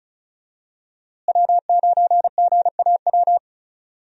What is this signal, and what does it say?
Morse code, heard as a keyed tone, about 34 wpm, 700 Hz. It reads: W9GAW
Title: USA Call Signs - 5 Characters - Repeated 2x - 35wpm